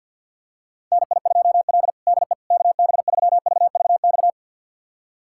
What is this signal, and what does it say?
Morse code, heard as a keyed tone, about 51 wpm, 690 Hz. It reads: NI2L DE KB3VVX
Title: USA Callsign DE Callsign 50wpm